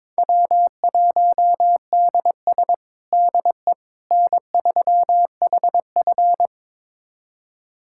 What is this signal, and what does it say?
Morse code, heard as a keyed tone, about 21 wpm, 700 Hz. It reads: W1DS DE N3HF
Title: USA Callsign DE Callsign - 4 Characters - 22wpm